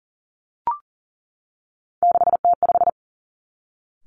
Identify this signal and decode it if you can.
Morse code, heard as a keyed tone, about 42 wpm, 700 Hz. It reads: 6T5
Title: Three Challenging Letter-Number Combinations 40wpm